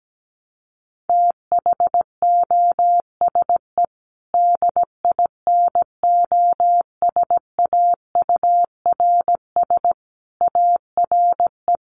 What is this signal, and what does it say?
Morse code, heard as a keyed tone, about 17 wpm, 700 Hz. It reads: THOSE DINOSAURS ARE
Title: A Sound Of Thunder 17wpm